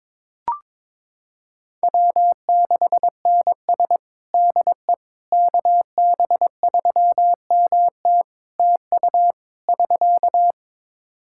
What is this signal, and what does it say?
Morse code, heard as a keyed tone, about 22 wpm, 700 Hz. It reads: W6NS DE KB3MT TU <SK>
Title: Practice QSO - Exchange 4 - Ending - 22wpm